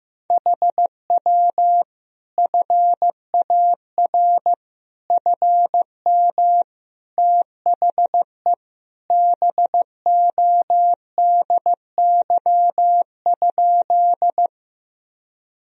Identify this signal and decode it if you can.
Morse code, heard as a keyed tone, about 16 wpm, 700 Hz. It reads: HW FAR FM THE BODY?
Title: Sentences from Top 400 Words and Abbreviations 15wpm